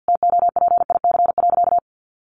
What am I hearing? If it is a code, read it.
Morse code, heard as a keyed tone, about 50 wpm, 700 Hz. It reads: TOPIC.
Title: Sentences from Top 1200 Words 50wpm